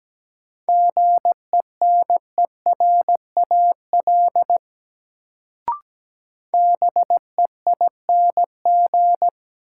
Morse code, heard as a keyed tone, about 17 wpm, 700 Hz. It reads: GENERAL BEING
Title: Sentences from Top 600 Words - New Word Drill - 17wpm